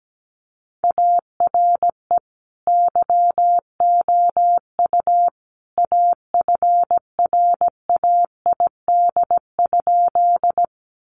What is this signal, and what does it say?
Morse code, heard as a keyed tone, about 17 wpm, 700 Hz. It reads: ARE YOU AFRAID?
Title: Sentences from Top 1500 Words 17wpm